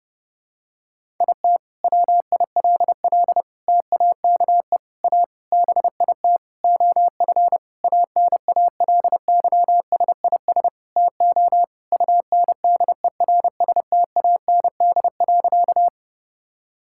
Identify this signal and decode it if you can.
Morse code, heard as a keyed tone, about 30 wpm, 700 Hz. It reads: IT WILL TAKE A BIT OF ANALYSIS TO UNDERSTAND.
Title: Sentences from Top 1000 Words - Review Part 5 - 30wpm